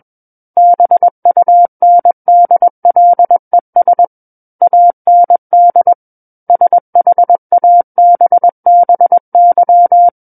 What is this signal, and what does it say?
Morse code, heard as a keyed tone, about 21 wpm, 700 Hz. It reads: BUNDLES AND SHABBY
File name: Book One / 12-Chapter 12 - 21 WPM